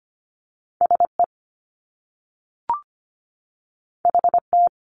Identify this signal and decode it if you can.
Morse code, heard as a keyed tone, about 25 wpm, 700 Hz. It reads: SE HT
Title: Top Two Letter Combinations in Words 25wpm